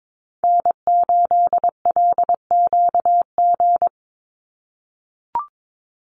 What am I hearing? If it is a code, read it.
Morse code, heard as a keyed tone, about 22 wpm, 700 Hz. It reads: N8LQG
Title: USA Call Signs 22wpm